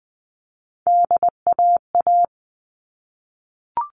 Morse code, heard as a keyed tone, about 20 wpm, 700 Hz. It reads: DAA
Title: All Cut Numbers - 3-Digit 20wpm